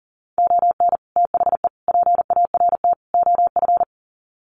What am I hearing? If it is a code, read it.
Morse code, heard as a keyed tone, about 40 wpm, 700 Hz. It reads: ON THE PART OF
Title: Sets of 4 Words 40wpm